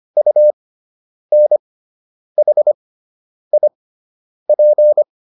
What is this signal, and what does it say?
Morse code, heard as a keyed tone, about 25 wpm, spaced as for 9 wpm, 600 Hz. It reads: UNHIP